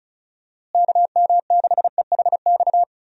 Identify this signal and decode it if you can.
Morse code, heard as a keyed tone, about 34 wpm, 700 Hz. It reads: KM6EHX
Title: USA Call Signs - 6 Characters - Repeated 2x - 35wpm